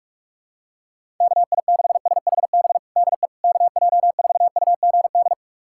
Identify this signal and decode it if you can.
Morse code, heard as a keyed tone, about 46 wpm, 700 Hz. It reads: KI6SHB DE KJ4UGD
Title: USA Callsign DE Callsign 45wpm